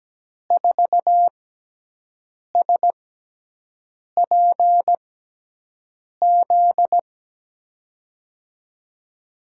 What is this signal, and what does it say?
Morse code, heard as a keyed tone, about 17 wpm, spaced as for 6 wpm, 700 Hz. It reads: 4SPZ
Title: USA Call Signs - 5 Characters - 6x Character Spacing - 17wpm